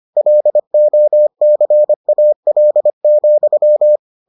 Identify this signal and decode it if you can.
Morse code, heard as a keyed tone, about 25 wpm, 600 Hz. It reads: LOCAL,